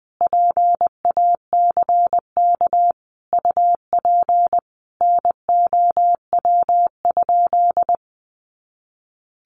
Morse code, heard as a keyed tone, about 19 wpm, 700 Hz. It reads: PACK UP NOW?
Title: Sentences from Top 1300 Words 20wpm